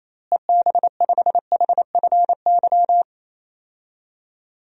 Morse code, heard as a keyed tone, about 28 wpm, 700 Hz. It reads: EB5HFY
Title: DXCC-100 Call Signs - 6 Characters - Repeated 2x - 28wpm